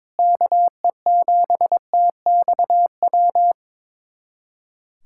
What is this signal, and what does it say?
Morse code, heard as a keyed tone, about 22 wpm, 700 Hz. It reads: KE7TXW
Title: USA Callsign DE Callsign 22wpm